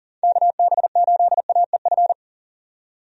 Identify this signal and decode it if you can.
Morse code, heard as a keyed tone, about 40 wpm, 700 Hz. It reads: KB8AEF
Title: USA Call Signs - 6 Characters - 40wpm